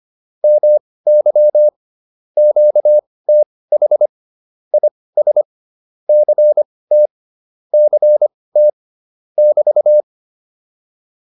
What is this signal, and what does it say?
Morse code, heard as a keyed tone, about 25 wpm, spaced as for 18 wpm, 600 Hz. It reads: MY QTH IS CT CT <BT>